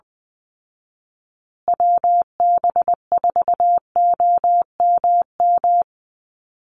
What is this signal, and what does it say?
Morse code, heard as a keyed tone, about 20 wpm, 700 Hz. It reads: WB4OMM
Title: Call Signs Repeated 2x - 20wpm